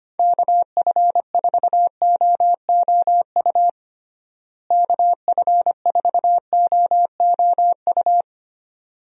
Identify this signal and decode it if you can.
Morse code, heard as a keyed tone, about 25 wpm, 700 Hz. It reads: KF4OOU KF4OOU
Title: USA Call Signs - 6 Characters - Repeated 2x - 25wpm